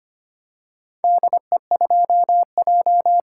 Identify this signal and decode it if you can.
Morse code, heard as a keyed tone, about 25 wpm, 700 Hz. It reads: DE2J
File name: practice / 25wpm / Four Letter-Number Combinations - 25wpm